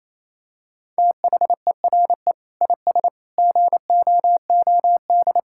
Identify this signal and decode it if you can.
Morse code, heard as a keyed tone, about 28 wpm, 700 Hz. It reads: THERE IS GOOD
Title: Sentences from Top 1000 Words - Review Part 5 - 28wpm